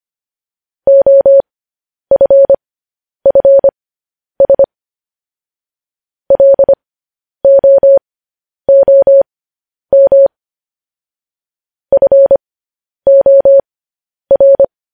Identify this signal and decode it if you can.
Morse code, heard as a keyed tone, about 25 wpm, spaced as for 10 wpm, 550 Hz. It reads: OFFS LOOM FOR
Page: news-headlines-in-morse-code-at-10-wpm